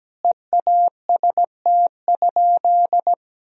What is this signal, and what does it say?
Morse code, heard as a keyed tone, about 17 wpm, 700 Hz. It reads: EAST?
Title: Sentences from Top 1000 Words - Review Part 3 - 17wpm